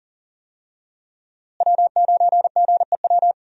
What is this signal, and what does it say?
Morse code, heard as a keyed tone, about 40 wpm, 700 Hz. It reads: W9GEW
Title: USA Call Signs - 5 Characters - Repeated 2x - 40wpm